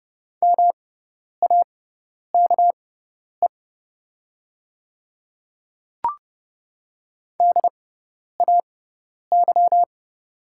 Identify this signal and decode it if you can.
Morse code, heard as a keyed tone, about 30 wpm, spaced as for 10 wpm, 700 Hz. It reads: MAKE DAY
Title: 6x Character Spacing - Top 200 Words - 30wpm